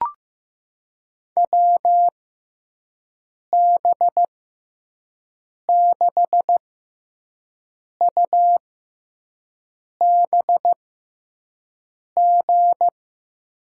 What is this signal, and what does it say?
Morse code, heard as a keyed tone, about 15 wpm, spaced as for 5 wpm, 700 Hz. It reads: WB6UBG
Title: USA Call Signs - 6x Character Spacing - 15wpm